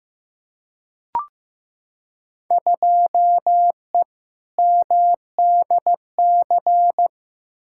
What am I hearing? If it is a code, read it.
Morse code, heard as a keyed tone, about 15 wpm, 700 Hz. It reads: 2E MDC
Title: Field Day Exchanges - 15wpm